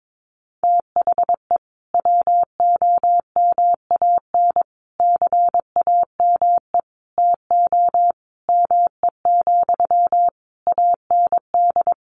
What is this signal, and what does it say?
Morse code, heard as a keyed tone, about 22 wpm, 700 Hz. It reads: THE WOMAN CAME TO ME, AND
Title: Sentences from Top 1100 Words 22wpm